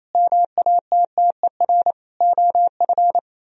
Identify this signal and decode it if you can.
Morse code, heard as a keyed tone, about 32 wpm, 700 Hz. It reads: MATTER OF
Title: Speed-Racing - Sets of 4 Words 28wpm